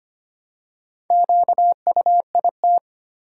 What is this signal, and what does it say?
Morse code, heard as a keyed tone, about 25 wpm, 700 Hz. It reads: QUIT
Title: Four Letter Words - 25wpm